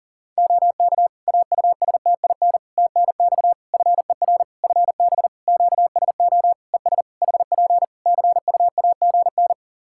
Morse code, heard as a keyed tone, about 40 wpm, 700 Hz. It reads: OK AUSTIN TNX FER FB QSO ES HP CUAGN
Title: Practice QSO - Exchange 4 - Ending - 40wpm